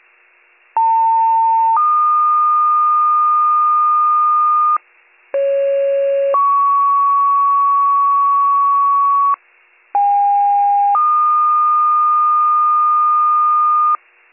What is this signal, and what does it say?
Two-tone page, heard as a one-second tone, then a three-second tone, about 900 then 1220 Hz, about 550 then 1120 Hz, about 810 then 1220 Hz. It is heard from 0.8 to 14.0 s.